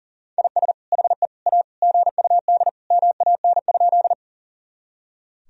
Morse code, heard as a keyed tone, about 40 wpm, 700 Hz. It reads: IS HE A GUD MAN?
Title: Sentences from Top 200 Words and Abbreviations 40wpm